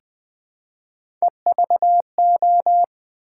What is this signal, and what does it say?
Morse code, heard as a keyed tone, about 20 wpm, 700 Hz. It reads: EVO
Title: Build A Word - 1 Letter At A Time - 5-Letter Words - 20wpm